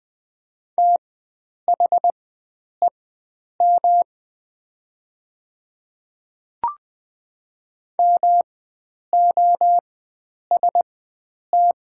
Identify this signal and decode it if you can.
Morse code, heard as a keyed tone, about 20 wpm, spaced as for 9 wpm, 690 Hz. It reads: THEM MOST
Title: 4x Character Spacing - Top 100 Words - 20wpm